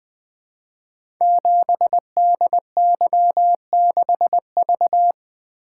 Morse code, heard as a keyed tone, about 20 wpm, 700 Hz. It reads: 7DY6V
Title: Code Groups - 5 Letter:Numbers - 20wpm